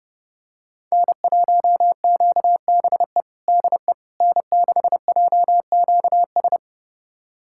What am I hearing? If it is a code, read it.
Morse code, heard as a keyed tone, about 30 wpm, 700 Hz. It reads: N1QBE DE N6JQS
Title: USA Callsign DE Callsign - 5 Characters - 30wpm